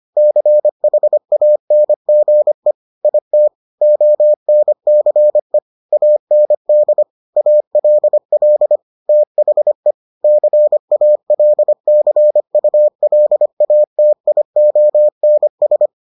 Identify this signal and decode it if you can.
Morse code, heard as a keyed tone, about 25 wpm, 600 Hz. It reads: CHANGE IT ONCE AND ALL THE CALCULATIONS